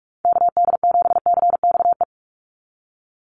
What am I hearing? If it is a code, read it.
Morse code, heard as a keyed tone, about 44 wpm, 700 Hz. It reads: KD7CXE
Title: USA Call Signs 45wpm